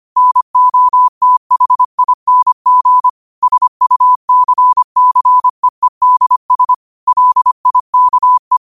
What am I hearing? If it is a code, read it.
Morse code, heard as a keyed tone, about 25 wpm, 1000 Hz. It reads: NOTHING SUCCEEDS LIKE
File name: qotdfiles / 25wpm